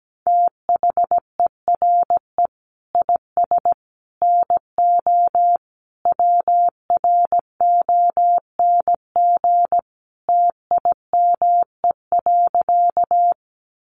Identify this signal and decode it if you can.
Morse code, heard as a keyed tone, about 17 wpm, 700 Hz. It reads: THERE IS NO WRONG TIME.
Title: Sentences from Top 900 Words 17wpm